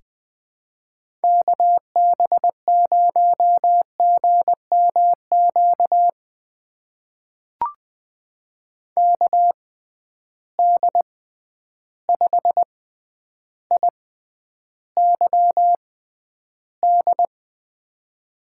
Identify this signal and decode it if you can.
Morse code, heard as a keyed tone, about 20 wpm, spaced as for 10 wpm, 700 Hz. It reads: KB0GMQ KD5IYD
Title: USA Call Signs - 6 Characters - 6x Character Spacing - 20wpm